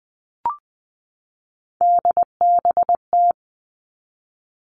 Morse code, heard as a keyed tone, about 20 wpm, 700 Hz. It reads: DBT
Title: All Cut Numbers - 3-Digit 20wpm